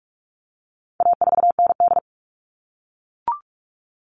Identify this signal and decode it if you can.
Morse code, heard as a keyed tone, about 44 wpm, 700 Hz. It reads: A4ND
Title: All Cut Numbers - 4-Digit 45wpm